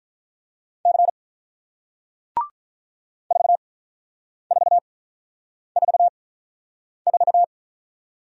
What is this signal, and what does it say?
Morse code, heard as a keyed tone, about 44 wpm, spaced as for 9 wpm, 700 Hz. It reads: C 4444